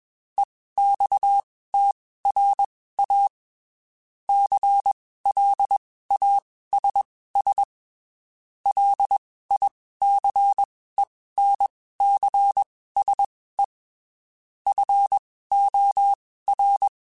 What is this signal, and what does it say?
Morse code, heard as a keyed tone, about 21 wpm, spaced as for 15 wpm, 800 Hz. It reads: EXTRA CLASS LICENCSE FOR
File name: (spaced) / QSOs Spaced-mp3s / kc7sac @20 sp